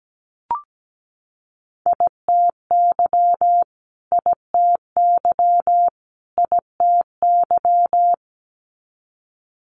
Morse code, heard as a keyed tone, about 17 wpm, 700 Hz. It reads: ITY ITY ITY
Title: Multi-letter Phonemes Repeated 3x 17wpm